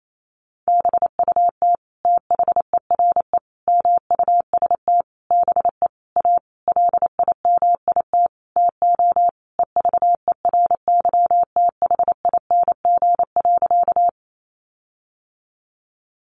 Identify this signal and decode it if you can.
Morse code, heard as a keyed tone, about 28 wpm, 700 Hz. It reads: BUT THERE MUST BE A LIMIT TO EVERYTHING.
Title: Sentences from Top 1500 Words 28wpm